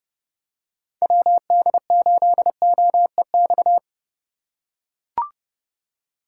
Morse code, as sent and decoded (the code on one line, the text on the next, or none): .-- -.. ---.. --- . -..-
WD8OEX